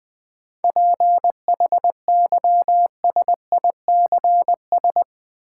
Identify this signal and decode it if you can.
Morse code, heard as a keyed tone, about 20 wpm, 700 Hz. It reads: PHYSICS
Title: Sentences from Top 1000 Words - Review Part 5 - New Word Drill - 20wpm